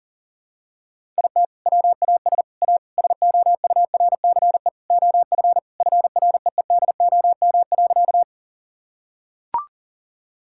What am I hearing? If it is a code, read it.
Morse code, heard as a keyed tone, about 40 wpm, 700 Hz. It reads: IT WAS A SOURCE OF FREEDOM.
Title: Sentences from Top 1000 Words - Review Part 5 - 40wpm